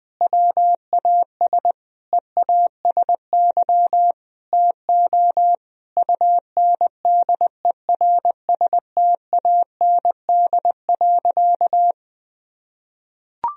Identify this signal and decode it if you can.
Morse code, heard as a keyed tone, about 20 wpm, 700 Hz. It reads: WAS EASY TO UNDERSTAND.
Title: Sentences from Top 1000 Words - Review Part 3 - 20wpm